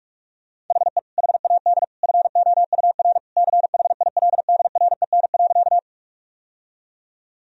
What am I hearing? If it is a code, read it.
Morse code, heard as a keyed tone, about 46 wpm, 700 Hz. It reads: HE HAD FOUR CHILDREN.
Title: Sentences from Top 1000 Words - Review Part 1 - 45wpm